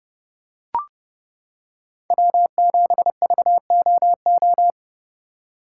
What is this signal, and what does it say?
Morse code, heard as a keyed tone, about 29 wpm, 700 Hz. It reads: W7VOO